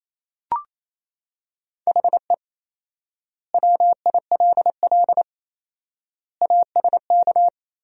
Morse code, heard as a keyed tone, about 28 wpm, 700 Hz. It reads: HE WILL ASK
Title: 4x Word-Spacing - Sentences from Top 200 Words - 28wpm